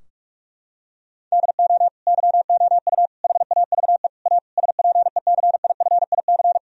Morse code, heard as a keyed tone, about 46 wpm, 700 Hz. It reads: DO YOU HAVE A SPECIFIC
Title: Sentences from Top 1000 Words - Review Part 3 - 45wpm